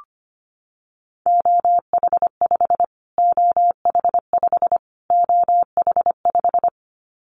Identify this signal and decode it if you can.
Morse code, heard as a keyed tone, about 25 wpm, 700 Hz. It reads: OH5 OH5 OH5
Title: Callsign Prefixes for Top 25 DXCC - Repeated 3x - 25wpm